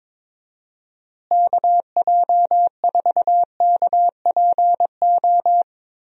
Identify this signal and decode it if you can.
Morse code, heard as a keyed tone, about 22 wpm, 700 Hz. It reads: KJ4KPO
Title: USA Call Signs - 6 Characters - Repeated 2x - 22wpm